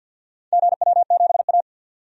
Morse code, heard as a keyed tone, about 47 wpm, 700 Hz. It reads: GW7A
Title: DXCC-100 Call Signs - 4 Characters - 50wpm